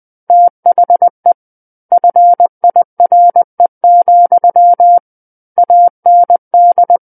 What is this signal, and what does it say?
Morse code, heard as a keyed tone, about 20 wpm, 700 Hz. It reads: THE FIRE, AND
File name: 20 WPM